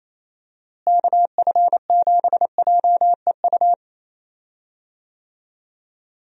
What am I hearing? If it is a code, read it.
Morse code, heard as a keyed tone, about 28 wpm, 700 Hz. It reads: KF7JEU
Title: USA Call Signs - 6 Characters - Repeated 3x - 28wpm